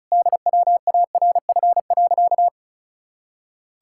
Morse code, heard as a keyed tone, about 36 wpm, 700 Hz. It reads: DWARF.
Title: Pangram Challenge v2 - 35wpm